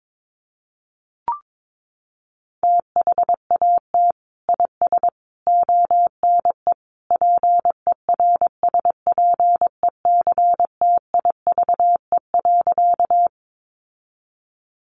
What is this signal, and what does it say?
Morse code, heard as a keyed tone, about 22 wpm, 700 Hz. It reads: THAT IS ONE PERSPECTIVE.
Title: Sentences from Top 1200 Words 22wpm